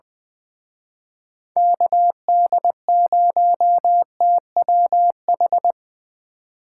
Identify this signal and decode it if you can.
Morse code, heard as a keyed tone, about 23 wpm, 700 Hz. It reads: KD0TWH
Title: Speed Racing - USA 2x3 Call Signs 20wpm